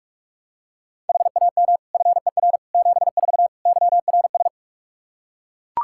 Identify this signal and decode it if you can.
Morse code, heard as a keyed tone, about 41 wpm, 700 Hz. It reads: HAM FER 74 YRS
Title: Practice QSO - Exchange 3 - More Info - 45wpm